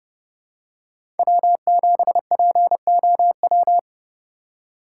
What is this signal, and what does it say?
Morse code, heard as a keyed tone, about 30 wpm, 700 Hz. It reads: W7POW
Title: USA Call Signs - 5 Characters - 4x Character Spacing - 30wpm